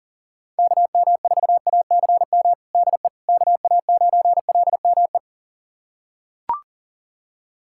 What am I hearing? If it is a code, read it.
Morse code, heard as a keyed tone, about 40 wpm, 700 Hz. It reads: KM4ACM DE KA9LME
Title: USA Callsign DE Callsign 40wpm